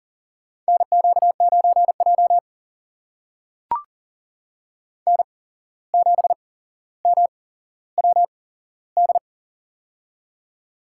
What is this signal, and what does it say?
Morse code, heard as a keyed tone, about 40 wpm, spaced as for 16 wpm, 700 Hz. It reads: NQ9J N7MWD
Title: USA Call Signs - 8x Character Spacing - 40wpm